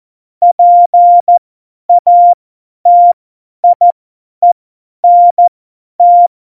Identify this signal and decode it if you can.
Morse code, heard as a keyed tone, about 14 wpm, spaced as for 10 wpm, 700 Hz. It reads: PATIENT
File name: Morsekurs / Witze / Witz 09. Januar 2018, 50 F70